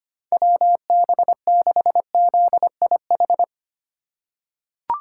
Morse code, heard as a keyed tone, about 25 wpm, 700 Hz. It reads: WB6ZIH
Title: USA Call Signs - 6 Characters - 3x Character Spacing - 25wpm